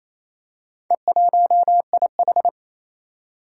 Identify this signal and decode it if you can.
Morse code, heard as a keyed tone, about 28 wpm, 700 Hz. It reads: E1IH
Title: Four Challenging Letter-Number Combinations 28wpm